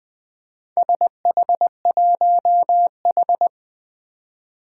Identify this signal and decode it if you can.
Morse code, heard as a keyed tone, about 21 wpm, 700 Hz. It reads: SH1H
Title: Four Challenging Letter-Number Combinations 20wpm